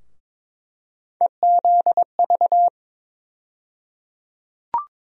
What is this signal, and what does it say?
Morse code, heard as a keyed tone, about 22 wpm, 700 Hz. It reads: EZV
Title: USA Call Sign Postfixes - 3 Letters 22wpm